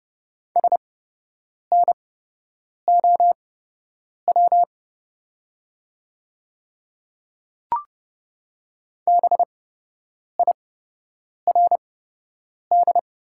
Morse code, heard as a keyed tone, about 30 wpm, spaced as for 8 wpm, 700 Hz. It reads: SNOW BIRD
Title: 8x Character Spacing - Top 500 Words - 30wpm